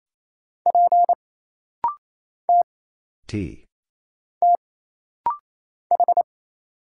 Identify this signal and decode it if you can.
Morse code, heard as a keyed tone, about 27 wpm, 700 Hz. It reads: P T T H
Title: Single Letter-Number - Rapid-Fire 28wpm